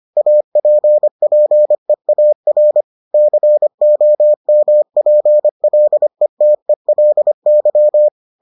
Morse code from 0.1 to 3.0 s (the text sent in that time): APPEAR